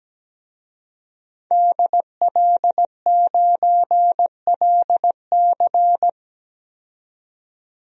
Morse code, heard as a keyed tone, about 17 wpm, 700 Hz. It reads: DL9LC
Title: DXCC-100 Call Signs - 5 Characters - 17wpm